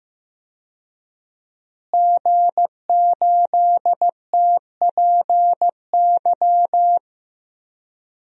Morse code, heard as a keyed tone, about 16 wpm, 700 Hz. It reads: G8TPY
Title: DXCC-100 Call Signs - 5 Characters - Repeated 3x - 15wpm